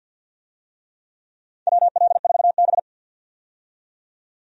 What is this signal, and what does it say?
Morse code, heard as a keyed tone, about 48 wpm, 700 Hz. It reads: WL4B
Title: USA Call Signs - 4 Characters - 50wpm